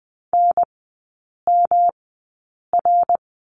Morse code, heard as a keyed tone, about 20 wpm, spaced as for 8 wpm, 690 Hz. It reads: NMR
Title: 212 Lesson 34e - Number 8 - Three Characters 20wpm